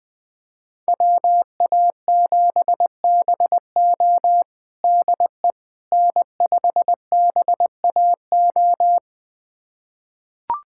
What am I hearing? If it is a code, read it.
Morse code, heard as a keyed tone, about 20 wpm, 700 Hz. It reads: WA7BO DE N5BAO
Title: USA Callsign DE Callsign - 5 Characters - 20wpm